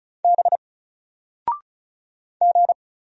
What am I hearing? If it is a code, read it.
Morse code, heard as a keyed tone, about 36 wpm, 700 Hz. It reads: B G
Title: Single Letter-Number 35wpm